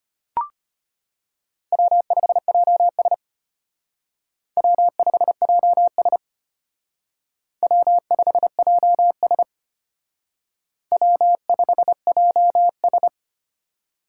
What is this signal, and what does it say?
Morse code, heard as a keyed tone, about 32 wpm, 700 Hz. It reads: W5JS W5JS W5JS W5JS